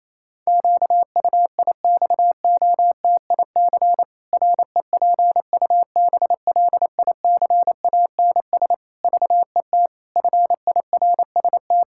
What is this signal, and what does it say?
Morse code, heard as a keyed tone, about 27 wpm, 700 Hz. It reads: QUIXOTIC REPUBLICANS VET FIRST
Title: Pangram Challenge v2 - 28wpm